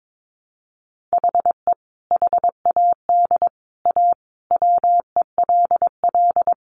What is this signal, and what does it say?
Morse code, heard as a keyed tone, about 22 wpm, 700 Hz. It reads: HE HAD A WELL
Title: Sentences from Top 1300 Words 22wpm